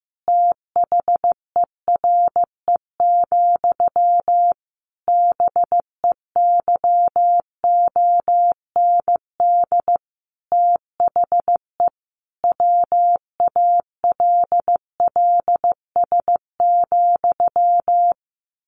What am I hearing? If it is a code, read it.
Morse code, heard as a keyed tone, about 15 wpm, 700 Hz. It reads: THERE, BEYOND THE WALLS,
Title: A Sound Of Thunder 15wpm